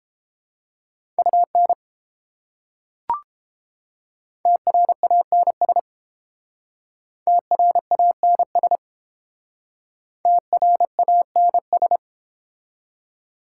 Morse code, heard as a keyed tone, about 30 wpm, 700 Hz. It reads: UN TRANS TRANS TRANS